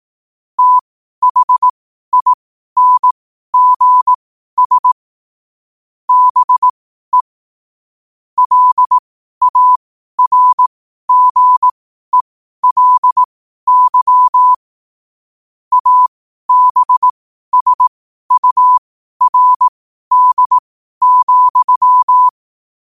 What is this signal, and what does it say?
Morse code, heard as a keyed tone, about 18 wpm, spaced as for 12 wpm, 1000 Hz. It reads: THINGS BE LARGELY ABSURD,
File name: qotdfiles / 12wpm